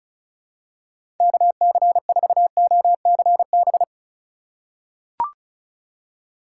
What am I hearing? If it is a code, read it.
Morse code, heard as a keyed tone, about 36 wpm, 700 Hz. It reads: KC4OCB
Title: USA Call Signs - 6 Characters - 3x Character Spacing - 35wpm